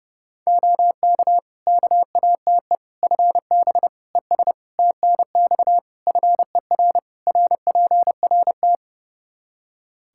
Morse code, heard as a keyed tone, about 30 wpm, 700 Hz. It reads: OK KATE FB ES TNX FER RPRT
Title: Practice QSO - Exchange 2 - Info - 30wpm